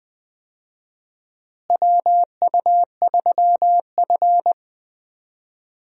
Morse code, heard as a keyed tone, about 20 wpm, 700 Hz. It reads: WU3F
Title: USA Call Signs - 4 Characters - Repeated 3x - 20wpm